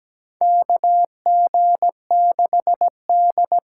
Morse code, heard as a keyed tone, about 17 wpm, 700 Hz. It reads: KG6D
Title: USA Call Signs - 6 Characters - 8x Character Spacing - 17wpm